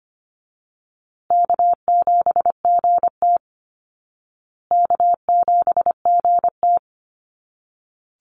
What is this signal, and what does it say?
Morse code, heard as a keyed tone, about 25 wpm, 700 Hz. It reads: K7GT K7GT
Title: Call Signs Repeated 2x - 25wpm